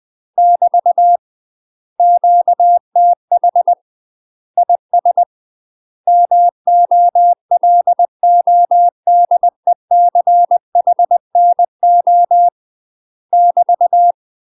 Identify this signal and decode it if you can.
Morse code, heard as a keyed tone, about 20 wpm, 700 Hz. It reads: <BT> QTH IS MOLODECHNO <BT>
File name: UK MASTER QSOs TRAINING / 20WPM / UK MASTER QSOs 0010